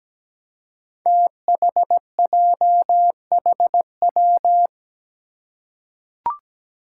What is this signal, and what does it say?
Morse code, heard as a keyed tone, about 17 wpm, 700 Hz. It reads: THJHW
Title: Five Challenging Letter-Number Combinations 17wpm